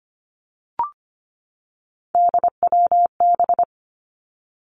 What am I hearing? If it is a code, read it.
Morse code, heard as a keyed tone, about 25 wpm, 700 Hz. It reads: DWB
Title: Three Challenging Letter-Number Combinations 25wpm